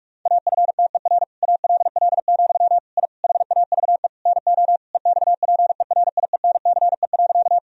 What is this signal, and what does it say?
Morse code, heard as a keyed tone, about 41 wpm, 700 Hz. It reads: AFTER ALL, I HAVE NO EXPERIENCE.
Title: Sentences from Top 1000 Words - Review Part 3 - 45wpm